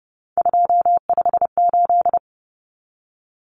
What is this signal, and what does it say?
Morse code, heard as a keyed tone, about 30 wpm, 700 Hz. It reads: 258
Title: Numbers - 3-Digit 30wpm